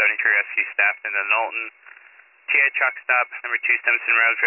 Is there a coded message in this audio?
Yes: Emergency Alert System (SAME) tones from 0.2 to 3.8 s, heard as harsh buzzing bursts, with speech.